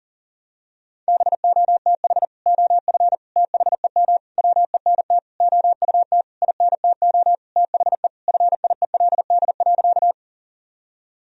Morse code, heard as a keyed tone, about 40 wpm, 700 Hz. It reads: BOTH OF THEM WENT OUT INTO THE FIELD.
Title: Sentences from Top 1000 Words - Review Part 2 - 40wpm